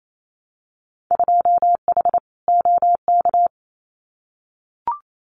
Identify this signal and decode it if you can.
Morse code, heard as a keyed tone, about 28 wpm, 700 Hz. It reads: 2H OK
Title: Winter Field Day Exchanges Encore 28wpm